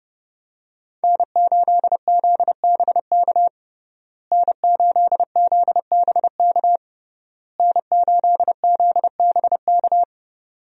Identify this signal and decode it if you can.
Morse code, heard as a keyed tone, about 31 wpm, 700 Hz. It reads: N8ZBK N8ZBK N8ZBK
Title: USA Call Signs - 5 Characters - Repeated 3x - 30wpm